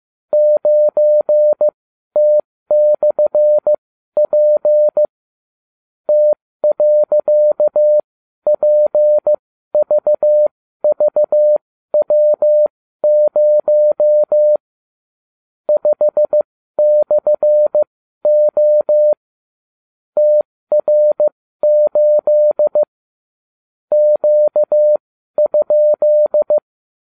Morse code, heard as a keyed tone, about 15 wpm, spaced as for 12 wpm, 600 Hz. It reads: 9T/P T.PVVW0 5/O TR8 Q?
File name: cw / ubacw31 1